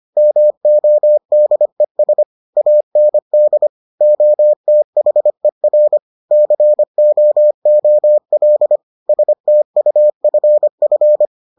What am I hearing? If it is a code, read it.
Morse code, heard as a keyed tone, about 25 wpm, 600 Hz. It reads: MODES AND OTHER COOL STUFF